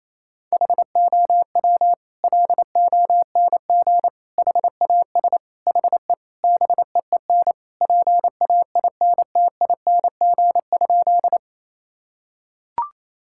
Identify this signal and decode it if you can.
Morse code, heard as a keyed tone, about 28 wpm, 700 Hz. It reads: HOW LONG HAS HE BEEN PAINTING?
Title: Sentences from Top 1000 Words - Review Part 6 - 28wpm